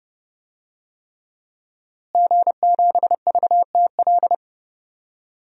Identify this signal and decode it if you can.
Morse code, heard as a keyed tone, about 31 wpm, 700 Hz. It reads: G7VTL
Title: DXCC-100 Call Signs - 5 Characters - Repeated 3x - 30wpm